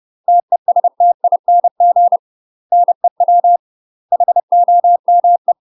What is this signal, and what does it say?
Morse code, heard as a keyed tone, about 30 wpm, 700 Hz. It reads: TESTING NEW HOME
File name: UK MASTER QSOs TRAINING / 30WPM / UK MASTER QSOs 0040